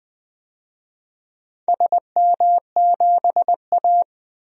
Morse code, heard as a keyed tone, about 20 wpm, 700 Hz. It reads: SM7A
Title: DXCC-100 Call Signs - 4 Characters - 20wpm